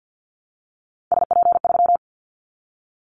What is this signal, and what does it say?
Morse code, heard as a keyed tone, about 50 wpm, 700 Hz. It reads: SP3
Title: Callsign Prefixes for Top 25 DXCC - 50wpm